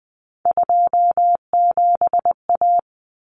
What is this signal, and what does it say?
Morse code, heard as a keyed tone, about 20 wpm, 700 Hz. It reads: 27A
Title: Common Cut Numbers - 3-Digit 20wpm